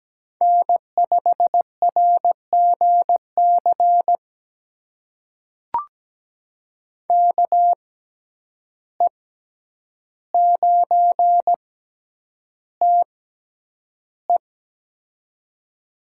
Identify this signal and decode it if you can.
Morse code, heard as a keyed tone, about 17 wpm, spaced as for 9 wpm, 700 Hz. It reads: N5RGC KE9TE
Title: USA Call Signs - 5 Characters - 6x Character Spacing - 17wpm